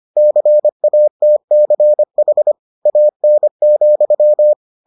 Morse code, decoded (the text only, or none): CATCH AN,